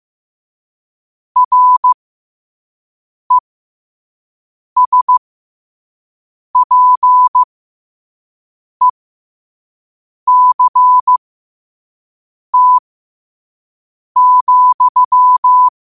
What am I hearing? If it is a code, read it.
Morse code, heard as a keyed tone, about 15 wpm, spaced as for 5 wpm, 1000 Hz. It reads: RESPECT,
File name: qotdfiles / 5wpm